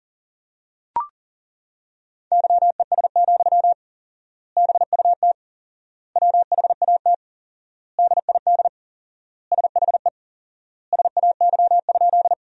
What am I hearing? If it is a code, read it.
Morse code, heard as a keyed tone, about 40 wpm, 700 Hz. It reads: YES, BUT WHAT DID SHE SAY?
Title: 4x Word-Spacing - Sentences from Top 500 Words - 40wpm